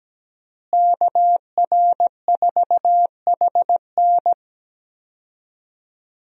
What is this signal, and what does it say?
Morse code, heard as a keyed tone, about 17 wpm, 700 Hz. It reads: KR4HN
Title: USA Call Signs - 5 Characters - Repeated 3x - 17wpm